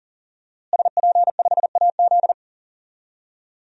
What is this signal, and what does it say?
Morse code, heard as a keyed tone, about 37 wpm, 700 Hz. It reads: SP5AZ
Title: DXCC-100 Call Signs - 5 Characters - Repeated 3x - 40wpm